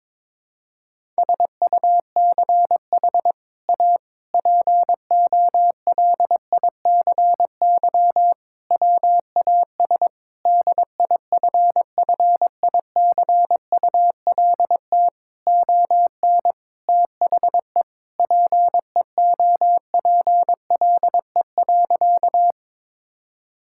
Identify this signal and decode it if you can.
Morse code, heard as a keyed tone, about 22 wpm, 700 Hz. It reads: SUCH A POLICY WAS DIFFICULT ON THE PEOPLE.
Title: Sentences from Top 1000 Words - Review Part 5 - 22wpm